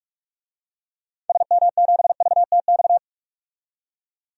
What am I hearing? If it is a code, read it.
Morse code, heard as a keyed tone, about 45 wpm, 700 Hz. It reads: SM7VTX
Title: DXCC-100 Call Signs - 6 Characters - 45wpm